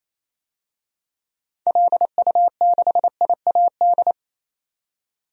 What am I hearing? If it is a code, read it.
Morse code, heard as a keyed tone, about 28 wpm, 700 Hz. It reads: LU6IAD
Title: DXCC-100 Call Signs - 6 Characters - 28wpm